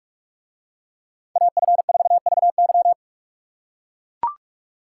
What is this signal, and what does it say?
Morse code, heard as a keyed tone, about 43 wpm, 700 Hz. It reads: AF4VY